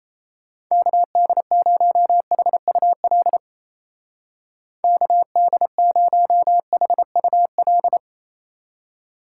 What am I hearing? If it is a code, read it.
Morse code, heard as a keyed tone, about 34 wpm, 700 Hz. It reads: KD0HUL KD0HUL
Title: Speed Racing - USA 2x3 Call Signs 28wpm